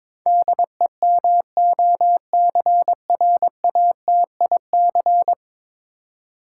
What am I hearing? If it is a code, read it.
Morse code, heard as a keyed tone, about 22 wpm, 700 Hz. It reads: DEMOCRATIC
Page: morse-code-ninja-practice-22wpm